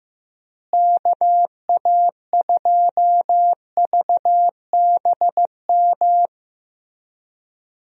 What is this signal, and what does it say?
Morse code, heard as a keyed tone, about 15 wpm, 700 Hz. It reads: KA2VBM
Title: USA Call Signs - 6 Characters - Repeated 2x - 15wpm